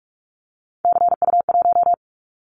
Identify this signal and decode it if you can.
Morse code, heard as a keyed tone, about 44 wpm, 700 Hz. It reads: CU1
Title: Callsign Prefixes for Top 100 DXCC - 45wpm